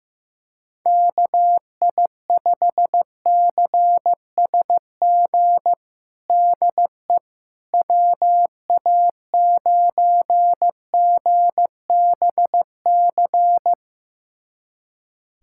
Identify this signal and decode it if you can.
Morse code, heard as a keyed tone, about 15 wpm, 700 Hz. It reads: KI5CSG DE WA9GBC
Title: USA Callsign DE Callsign - 6 Characters - 15wpm